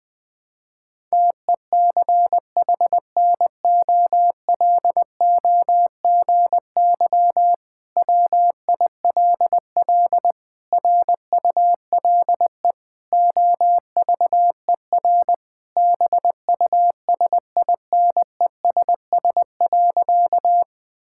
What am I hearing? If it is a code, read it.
Morse code, heard as a keyed tone, about 20 wpm, 700 Hz. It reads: TECHNOLOGY WILL RULE OVER BUSINESS.
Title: Sentences from Top 1000 Words - Review Part 5 - 20wpm